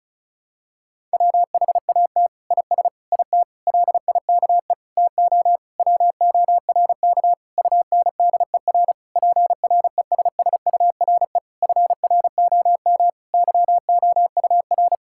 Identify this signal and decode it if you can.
Morse code, heard as a keyed tone, about 36 wpm, 700 Hz. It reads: WHAT IS IT LIKE TO WORK UNDER PRESSURE FROM YOUR
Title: Sentences from Top 1000 Words - Review Part 6 - 35wpm